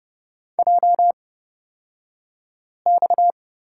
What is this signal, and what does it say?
Morse code, heard as a keyed tone, about 30 wpm, spaced as for 5 wpm, 700 Hz. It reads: JX
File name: Lesson 37d - Letter x - Two Characters 30wpm